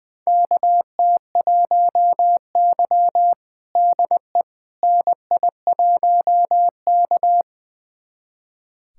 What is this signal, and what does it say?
Morse code, heard as a keyed tone, about 20 wpm, 700 Hz. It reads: KT1Y DE NI1K
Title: USA Callsign DE Callsign - 4 Characters - 20wpm